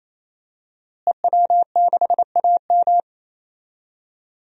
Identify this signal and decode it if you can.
Morse code, heard as a keyed tone, about 28 wpm, 700 Hz. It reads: EW6AM
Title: DXCC-100 Call Signs - 5 Characters - Repeated 3x - 28wpm